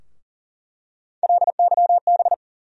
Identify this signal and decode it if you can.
Morse code, heard as a keyed tone, about 40 wpm, 700 Hz. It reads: LYB
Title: USA Call Sign Postfixes - 3 Letters 40wpm